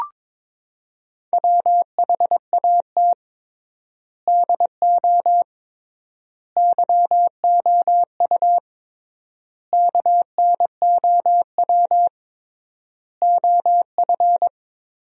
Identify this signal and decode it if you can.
Morse code, heard as a keyed tone, about 21 wpm, 700 Hz. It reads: WHAT DO YOU KNOW OF
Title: 3x Word-Spacing - Sentences from Top 100 Words - 22wpm